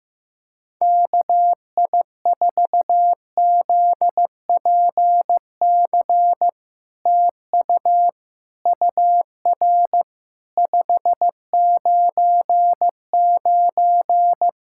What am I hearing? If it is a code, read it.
Morse code, heard as a keyed tone, about 15 wpm, 700 Hz. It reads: KI4ZPC TU UR 599
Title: SOTA QSO Practice 15wpm